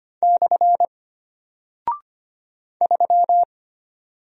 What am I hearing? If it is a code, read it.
Morse code, heard as a keyed tone, about 25 wpm, 700 Hz. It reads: / 3
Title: Single Character (Letter:Number:Punctuation:Prosign) Alt 25wpm